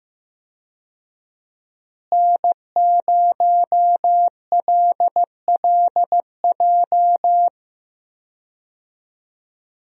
Morse code, heard as a keyed tone, about 15 wpm, 700 Hz. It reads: N0LLJ
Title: USA Call Signs - 5 Characters - Repeated 2x - 15wpm